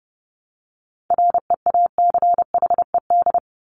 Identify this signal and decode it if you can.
Morse code, heard as a keyed tone, about 30 wpm, 700 Hz. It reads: REACHED
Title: ED Suffix - Top 100 Words with Base Words 30wpm